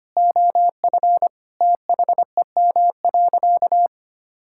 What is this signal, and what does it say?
Morse code, heard as a keyed tone, about 25 wpm, 700 Hz. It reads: OF THEM.
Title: Sentences from Top 1000 Words - Review Part 6 - 25wpm